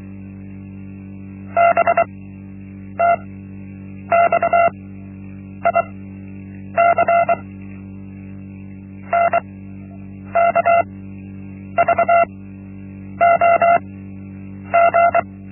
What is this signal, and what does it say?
Morse code, heard as a keyed tone, about 23 wpm, spaced as for 8 wpm, 680 Hz. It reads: BTXIC NKVOG
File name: Linksmakalnis-1